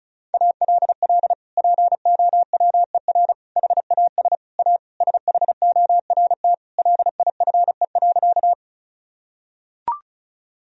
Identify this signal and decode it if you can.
Morse code, heard as a keyed tone, about 34 wpm, 700 Hz. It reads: ALL POWER HAS A SHORT LIFE.
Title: Sentences from Top 1000 Words - Review Part 2 - 35wpm